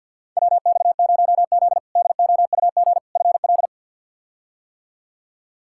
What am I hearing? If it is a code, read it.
Morse code, heard as a keyed tone, about 45 wpm, 700 Hz. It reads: WX0Z DOUG FL
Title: K1USN SST QSO Practice - 50wpm